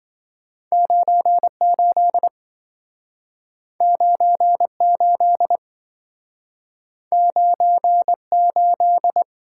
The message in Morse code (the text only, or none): 98 98 98